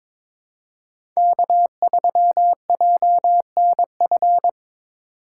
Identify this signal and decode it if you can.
Morse code, heard as a keyed tone, about 22 wpm, 700 Hz. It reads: K3JNF
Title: USA Call Signs - 5 Characters - 3x Character Spacing - 22wpm